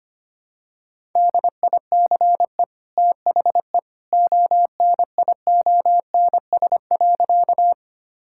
Morse code, heard as a keyed tone, about 25 wpm, 700 Hz. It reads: DICE THE ONIONS.
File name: Easy Sentences with Hints - 25wpm